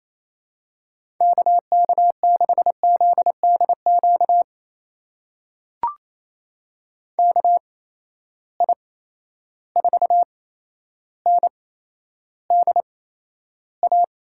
Morse code, heard as a keyed tone, about 28 wpm, spaced as for 12 wpm, 700 Hz. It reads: KK6ZDQ KI4NDA
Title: USA Call Signs - 6 Characters - 8x Character Spacing - 28wpm